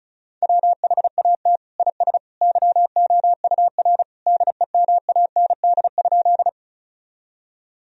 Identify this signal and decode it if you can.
Morse code, heard as a keyed tone, about 35 wpm, 700 Hz. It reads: WHAT IS YOUR DEMAND?
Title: Sentences from Top 1000 Words - Review Part 4 - 35wpm